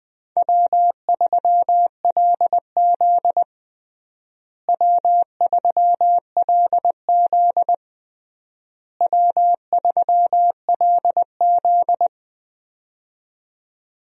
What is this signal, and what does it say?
Morse code, heard as a keyed tone, about 20 wpm, 700 Hz. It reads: W3LZ W3LZ W3LZ
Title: USA Call Signs - 4 Characters - Repeated 3x - 20wpm